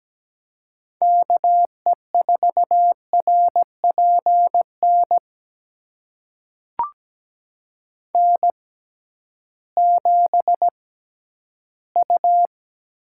Morse code, heard as a keyed tone, about 17 wpm, spaced as for 8 wpm, 700 Hz. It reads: KE4RP NN7U
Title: USA Call Signs - 6x Character Spacing - 17wpm